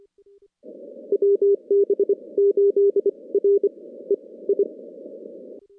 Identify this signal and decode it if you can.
Morse code, heard as a keyed tone, about 27 wpm, spaced as for 19 wpm, 410 Hz. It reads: WB8REI
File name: contests / CW-SS-k5tr- -2100